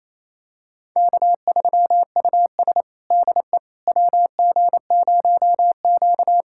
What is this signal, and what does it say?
Morse code, heard as a keyed tone, about 28 wpm, 700 Hz. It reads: K3US DE WG0Q
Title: USA Callsign DE Callsign - 4 Characters - 28wpm